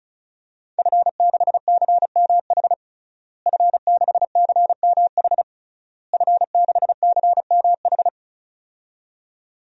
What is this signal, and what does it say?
Morse code, heard as a keyed tone, about 36 wpm, 700 Hz. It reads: F6CMH F6CMH F6CMH
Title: DXCC-100 Call Signs - 5 Characters - Repeated 3x - 35wpm